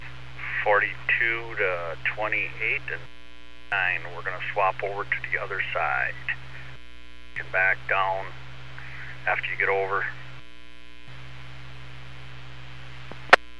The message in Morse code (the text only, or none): none